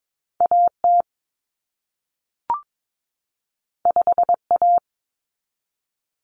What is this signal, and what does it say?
Morse code, heard as a keyed tone, about 22 wpm, 700 Hz. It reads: AT 5A
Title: Common Cut Numbers - 2-Digit 22wpm